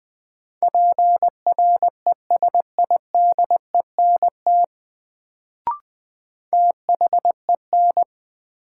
Morse code, heard as a keyed tone, about 20 wpm, 700 Hz. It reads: PRESIDENT THEN
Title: Sets of 2 Words - New Word Drill - 20wpm